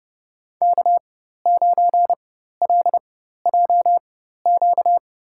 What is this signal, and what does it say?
Morse code, heard as a keyed tone, about 30 wpm, spaced as for 14 wpm, 700 Hz. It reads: K9LJQ
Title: USA Call Signs - 5 Characters - 4x Character Spacing - 30wpm